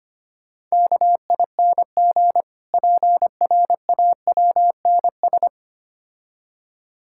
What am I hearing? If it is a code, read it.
Morse code, heard as a keyed tone, about 25 wpm, 690 Hz. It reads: KING PRAWNS
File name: practice / 25wpm / Difficult Words with Hints - 25wpm